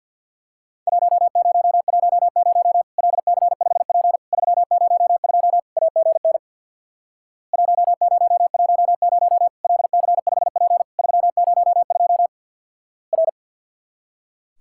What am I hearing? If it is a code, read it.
Morse code, heard as a keyed tone, about 50 wpm, 700 Hz. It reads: 1010 LC5P 302 AGN 1010 LC5P 302 R
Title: Worked All Europe DX Contest - QTC Exchange with Repeats - 50wpm